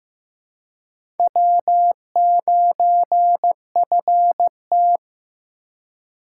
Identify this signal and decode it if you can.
Morse code, heard as a keyed tone, about 15 wpm, 700 Hz. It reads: W9FT
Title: USA Call Signs - 4 Characters - 4x Character Spacing - 15wpm